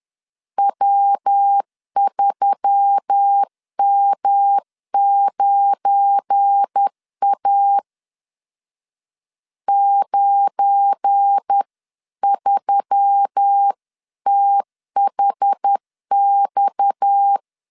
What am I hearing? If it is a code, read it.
Morse code, heard as a keyed tone, about 10 wpm, 800 Hz. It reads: W3M9A 93THX
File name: LEZIONI / 7 GF9Z / 7 GF9 10wpm 2747